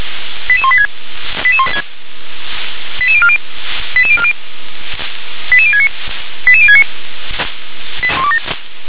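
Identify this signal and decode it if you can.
Five-tone selective call (ZVEI), heard as quick runs of five tones, tones of about 70 ms. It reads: 88186 88040 88040 88069 88069
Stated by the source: ZVEI five-tone selective call received on an SDR.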